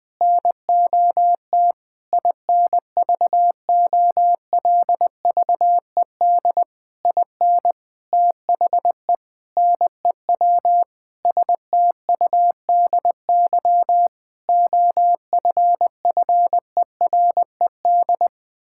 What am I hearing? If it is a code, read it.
Morse code, heard as a keyed tone, about 20 wpm, 700 Hz. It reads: NOT INVOLVED IN THE NEW STUDY OFFERED
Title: Sentences with 8-Word Phrases - 20wpm